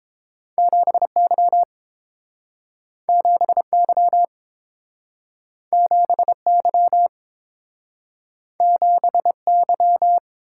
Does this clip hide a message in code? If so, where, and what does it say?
Yes: Morse code from 0.6 to 10.2 s, heard as a keyed tone, about 28 wpm, 700 Hz. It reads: 7Y 7Y 7Y 7Y